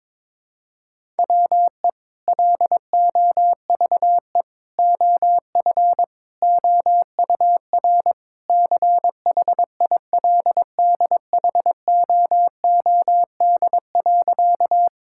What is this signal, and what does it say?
Morse code, heard as a keyed tone, about 22 wpm, 700 Hz. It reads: WE LOVE OF OUR CHILDHOOD.
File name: practice / 22wpm / Sentences with 2-Word Phrases - 22wpm